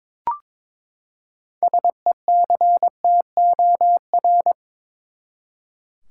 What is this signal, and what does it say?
Morse code, heard as a keyed tone, about 22 wpm, 700 Hz. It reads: SECTOR
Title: Six Letter Words - 22wpm